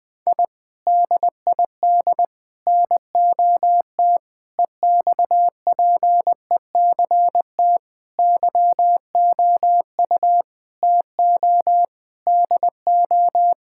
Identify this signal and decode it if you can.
Morse code, heard as a keyed tone, about 20 wpm, 700 Hz. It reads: I DID NOT EXPECT YOU TO DO
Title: Sentences with 3-Word Phrases - 20wpm